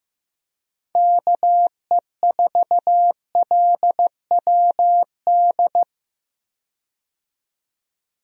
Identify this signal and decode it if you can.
Morse code, heard as a keyed tone, about 16 wpm, 700 Hz. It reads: KE4LWD
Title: USA Call Signs - 6 Characters - 15wpm